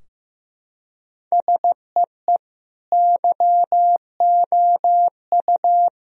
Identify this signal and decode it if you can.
Morse code, heard as a keyed tone, about 15 wpm, 700 Hz. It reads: SEE YOU
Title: Sentences from Top 1000 Words - Review Part 4 - 15wpm